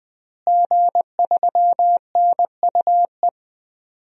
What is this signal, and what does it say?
Morse code, heard as a keyed tone, about 21 wpm, 700 Hz. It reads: G3NUE
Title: DXCC-100 Call Signs - 5 Characters - Repeated 3x - 20wpm